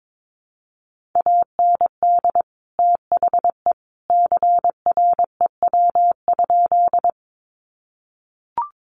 Morse code, heard as a keyed tone, about 22 wpm, 700 Hz. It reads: AND THE CREW?
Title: Sentences from Top 1500 Words 22wpm